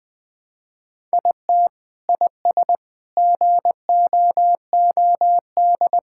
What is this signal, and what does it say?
Morse code, heard as a keyed tone, about 20 wpm, 700 Hz. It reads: IT IS GOOD
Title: Sentences with 3-Word Phrases - 20wpm